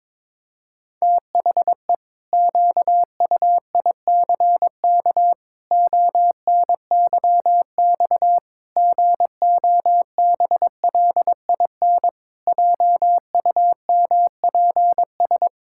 Morse code, heard as a keyed tone, about 22 wpm, 700 Hz. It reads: THE QUICK ONYX GOBLIN JUMPS
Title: Pangram Challenge v2 - 22wpm